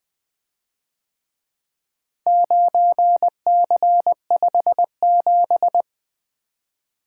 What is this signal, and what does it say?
Morse code, heard as a keyed tone, about 22 wpm, 700 Hz. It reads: 9C57
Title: Speed-Racing - Four Letter-Number Combinations - 20wpm